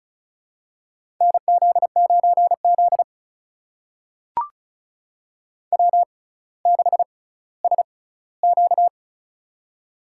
Morse code, heard as a keyed tone, about 36 wpm, spaced as for 18 wpm, 700 Hz. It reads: NZ9Z W6SQ
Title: USA Call Signs - 4 Characters - 6x Character Spacing - 35wpm